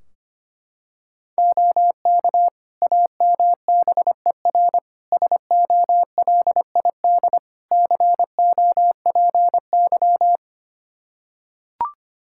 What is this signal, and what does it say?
Morse code, heard as a keyed tone, about 25 wpm, 700 Hz. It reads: OK AMBER SOLID COPY
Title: Practice QSO - Exchange 3 - More Info - 25wpm